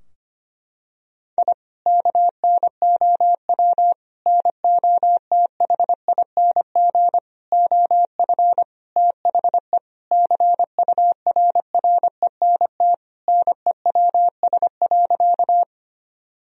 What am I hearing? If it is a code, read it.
Morse code, heard as a keyed tone, about 25 wpm, 700 Hz. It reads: I KNOW NOTHING OF THE CURRENT NEWS.
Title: Sentences from Top 1000 Words - Review Part 4 - 25wpm